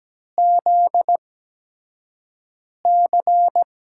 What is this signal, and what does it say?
Morse code, heard as a keyed tone, about 17 wpm, spaced as for 5 wpm, 700 Hz. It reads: ZC
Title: USA Call Signs - 4 Characters - 8x Character Spacing - 17wpm